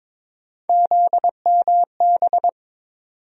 Morse code, heard as a keyed tone, about 22 wpm, 700 Hz. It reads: ZMB
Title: USA Call Sign Postfixes - 3 Letters 22wpm